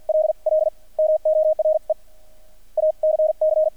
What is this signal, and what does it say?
Morse code, heard as a keyed tone, about 50 wpm, spaced as for 42 wpm, 620 Hz. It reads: LL MORE ACC
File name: archived / CRSnet